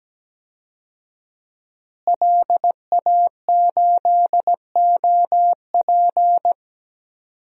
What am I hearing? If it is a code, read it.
Morse code, heard as a keyed tone, about 17 wpm, 700 Hz. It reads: LA8OP
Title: DXCC-100 Call Signs - 5 Characters - Repeated 3x - 17wpm